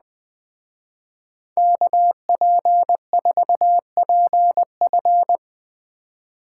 Morse code, heard as a keyed tone, about 20 wpm, 700 Hz. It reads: KP4PF